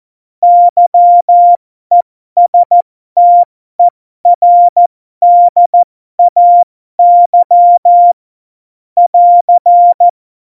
Morse code, heard as a keyed tone, about 14 wpm, spaced as for 12 wpm, 700 Hz. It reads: YESTERDAY <AR>